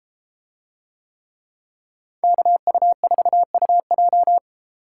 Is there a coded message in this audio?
Yes: Morse code from 2.2 to 4.4 s, heard as a keyed tone, about 32 wpm, 700 Hz. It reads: KU4UJ